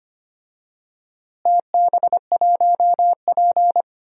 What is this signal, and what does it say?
Morse code, heard as a keyed tone, about 25 wpm, 700 Hz. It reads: TB1P
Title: DXCC-100 Call Signs - 4 Characters - Repeated 3x - 25wpm